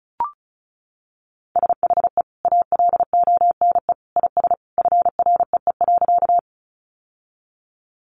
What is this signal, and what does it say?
Morse code, heard as a keyed tone, about 34 wpm, 700 Hz. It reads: SHE ALONE IS FREE.
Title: Sentences from Top 1000 Words 35wpm